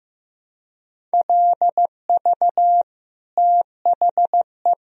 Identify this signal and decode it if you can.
Morse code, heard as a keyed tone, about 15 wpm, 700 Hz. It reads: LV THE
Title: Sentences from Top 500 Words and Abbreviations 15wpm